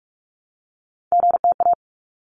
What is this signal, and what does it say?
Morse code, heard as a keyed tone, about 46 wpm, 700 Hz. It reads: GTA
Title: ARRL Sections 45wpm